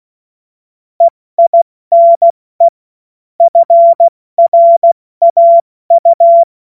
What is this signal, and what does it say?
Morse code, heard as a keyed tone, about 16 wpm, 680 Hz. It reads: EINE FRAU
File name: Morsekurs / Witze / Witz 12. Oktober 2021, 70 F80